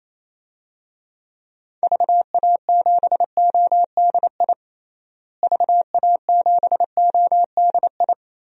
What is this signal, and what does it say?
Morse code, heard as a keyed tone, about 28 wpm, 700 Hz. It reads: VA7ODI VA7ODI
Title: DXCC-100 Call Signs - 6 Characters - Repeated 2x - 28wpm